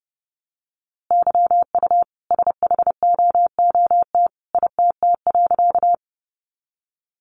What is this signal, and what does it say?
Morse code, heard as a keyed tone, about 30 wpm, 700 Hz. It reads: YU SHOOT ITT.
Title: A Sound Of Thunder 30wpm